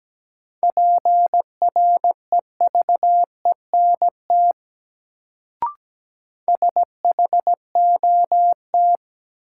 Morse code, heard as a keyed tone, about 17 wpm, 700 Hz. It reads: PREVENT SHOT
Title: Sentences from Top 1100 Words - New Word Drill - 17wpm